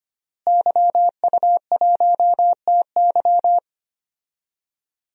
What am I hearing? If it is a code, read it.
Morse code, heard as a keyed tone, about 25 wpm, 700 Hz. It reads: YU1TY
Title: DXCC-100 Call Signs - 5 Characters - Repeated 2x - 25wpm